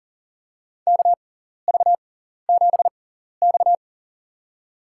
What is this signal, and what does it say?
Morse code, heard as a keyed tone, about 40 wpm, spaced as for 14 wpm, 700 Hz. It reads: KV7X
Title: USA Call Signs - 4 Characters - 6x Character Spacing - 40wpm